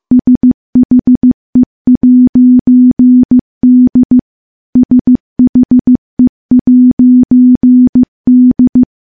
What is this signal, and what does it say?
Morse code, heard as a keyed tone, about 15 wpm, 250 Hz. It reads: SHE'D SHE'D
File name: morse / morse-training-15WPM